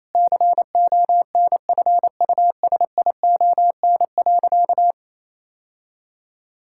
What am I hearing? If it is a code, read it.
Morse code, heard as a keyed tone, about 28 wpm, 700 Hz. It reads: CONFUSION.
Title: Sentences from Top 1600 Words 28wpm